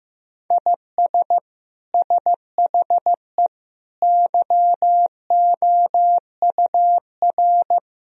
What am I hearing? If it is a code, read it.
Morse code, heard as a keyed tone, about 16 wpm, 700 Hz. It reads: IS SHE YOUR
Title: Sentences from Top 1000 Words - Review Part 3 - 15wpm